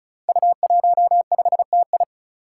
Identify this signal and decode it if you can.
Morse code, heard as a keyed tone, about 36 wpm, 700 Hz. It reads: U15TI
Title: Five Challenging Letter-Number Combinations 35wpm